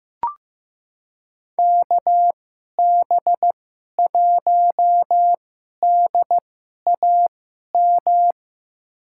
Morse code, heard as a keyed tone, about 15 wpm, spaced as for 11 wpm, 700 Hz. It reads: KB1DAM
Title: USA Call Signs - 6 Characters - 2x Character Spacing - 15wpm